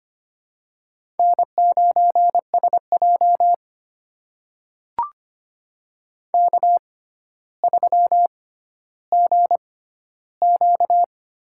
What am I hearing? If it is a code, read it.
Morse code, heard as a keyed tone, about 25 wpm, spaced as for 13 wpm, 700 Hz. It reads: N9SJ K3GQ
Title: USA Call Signs - 4 Characters - 6x Character Spacing - 25wpm